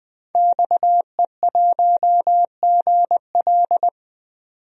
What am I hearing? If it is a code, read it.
Morse code, heard as a keyed tone, about 20 wpm, 700 Hz. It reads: XE1GL
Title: DXCC-100 Call Signs - 5 Characters - 20wpm